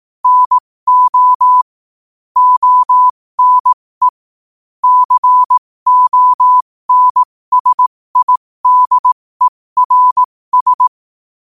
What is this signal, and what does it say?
Morse code, heard as a keyed tone, about 18 wpm, spaced as for 15 wpm, 1000 Hz. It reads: NO ONE CONSIDERS
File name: qotdfiles / 15wpm